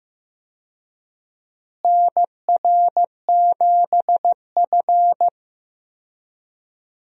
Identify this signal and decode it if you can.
Morse code, heard as a keyed tone, about 15 wpm, 700 Hz. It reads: NR7F
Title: USA Call Signs - 4 Characters - Repeated 2x - 15wpm